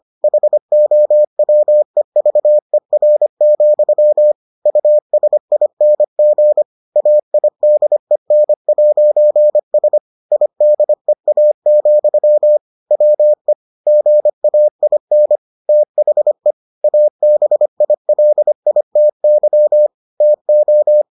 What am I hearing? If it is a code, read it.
Morse code, heard as a keyed tone, about 25 wpm, 600 Hz. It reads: HOWEVER, USING AIDEN'S IDEA, WE GAIN THE ABILITY TO